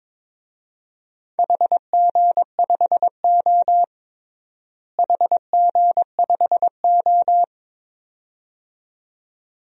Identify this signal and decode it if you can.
Morse code, heard as a keyed tone, about 22 wpm, 700 Hz. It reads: HG5O HG5O
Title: DXCC-100 Call Signs - 4 Characters - Repeated 2x - 22wpm